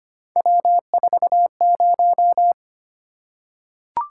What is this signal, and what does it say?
Morse code, heard as a keyed tone, about 25 wpm, 700 Hz. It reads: W40
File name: Three Letter-Number Combinations - 25wpm